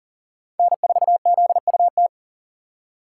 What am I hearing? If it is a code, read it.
Morse code, heard as a keyed tone, about 40 wpm, 700 Hz. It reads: N4ZUT
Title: USA Call Signs - 5 Characters - 4x Character Spacing - 40wpm